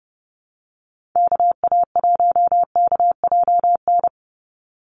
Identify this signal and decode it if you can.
Morse code, heard as a keyed tone, about 30 wpm, 700 Hz. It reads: KA1KJN
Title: USA Call Signs 30wpm